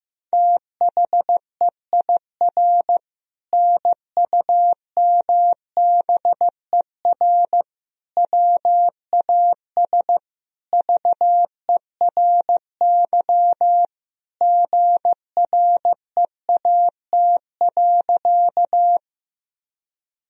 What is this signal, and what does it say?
Morse code, heard as a keyed tone, about 16 wpm, 700 Hz. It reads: THEIR NUMBER WAS VERY GREAT.
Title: Sentences from Top 1000 Words - Review Part 1 - 15wpm